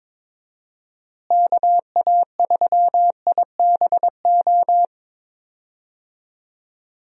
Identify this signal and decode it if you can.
Morse code, heard as a keyed tone, about 22 wpm, 700 Hz. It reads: KA3IBO
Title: USA Call Signs - 6 Characters - Repeated 2x - 22wpm